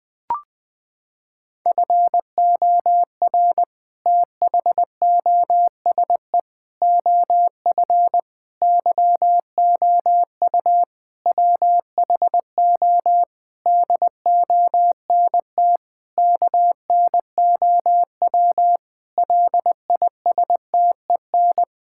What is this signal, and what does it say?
Morse code, heard as a keyed tone, about 20 wpm, 700 Hz. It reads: FOR THOSE OF YOU WHO DONT KNOW LISTEN
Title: Sentences with 7-Word Phrases - 20wpm